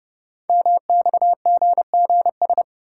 Morse code, heard as a keyed tone, about 30 wpm, 690 Hz. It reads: MXGGS
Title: Code Groups - 5 Letters - 30wpm